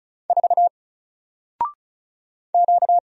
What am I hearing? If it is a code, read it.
Morse code, heard as a keyed tone, about 36 wpm, 700 Hz. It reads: 4 Q
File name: Single-Letter-Number-35wpm